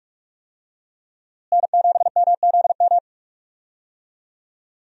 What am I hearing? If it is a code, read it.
Morse code, heard as a keyed tone, about 46 wpm, 700 Hz. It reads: N7MZM
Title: USA Call Signs - 5 Characters - Repeated 3x - 45wpm